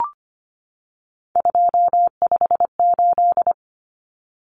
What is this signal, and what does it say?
Morse code, heard as a keyed tone, about 25 wpm, 700 Hz. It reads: 258
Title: Numbers - 3-Digit 25wpm